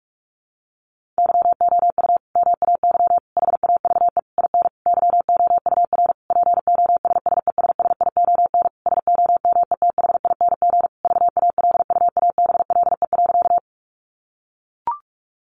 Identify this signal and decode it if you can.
Morse code, heard as a keyed tone, about 44 wpm, 700 Hz. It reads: YOU MAY HAVE IN YOUR POSSESSION SOMETHING VALUABLE.
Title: Sentences from Top 1400 Words 45wpm